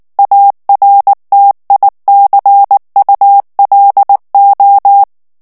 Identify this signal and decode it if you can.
Morse code, heard as a keyed tone, about 19 wpm, 800 Hz. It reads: ARTICULO
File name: ESPAGNOL TEXTE EN CLAIR 2 VITESSE 18 MOTS MINUTES